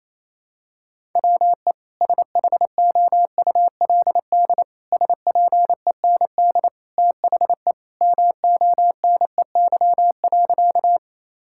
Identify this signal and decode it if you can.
Morse code, heard as a keyed tone, about 28 wpm, 700 Hz. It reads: WE SHOULD SPEND THE MONEY.
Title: Sentences from Top 1000 Words - Review Part 5 - 28wpm